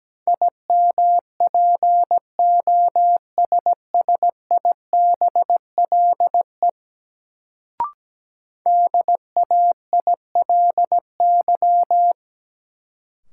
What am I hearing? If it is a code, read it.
Morse code, heard as a keyed tone, about 17 wpm, 700 Hz. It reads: IMPOSSIBLE DAILY
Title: Sentences from Top 1100 Words - New Word Drill - 17wpm